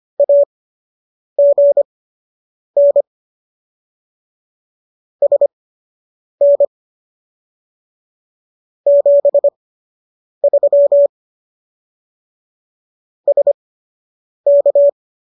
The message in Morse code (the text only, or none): AGN SN 73 SK